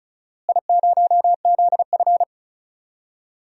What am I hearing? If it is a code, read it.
Morse code, heard as a keyed tone, about 36 wpm, 700 Hz. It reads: I0ZF
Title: DXCC-100 Call Signs - 4 Characters - Repeated 2x - 35wpm